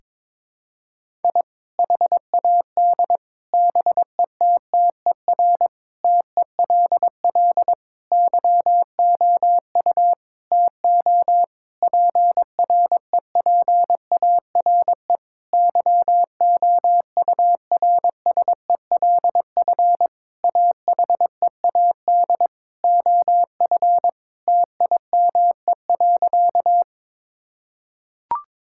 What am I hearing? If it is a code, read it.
Morse code, heard as a keyed tone, about 22 wpm, 700 Hz. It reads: I HAD BETTER TELL YOU TO PREPARE YOURSELF AHEAD OF TIME.
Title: Sentences from Top 1000 Words - Review Part 6 - 22wpm